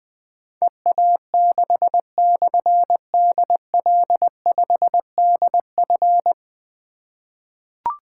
Morse code, heard as a keyed tone, about 20 wpm, 700 Hz. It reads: EA6/DL5DF
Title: Difficult Call Signs 20wpm